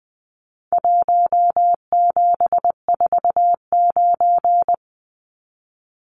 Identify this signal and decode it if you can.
Morse code, heard as a keyed tone, about 20 wpm, 700 Hz. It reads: 1749
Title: Numbers - Contest Style - 4-Digit 20wpm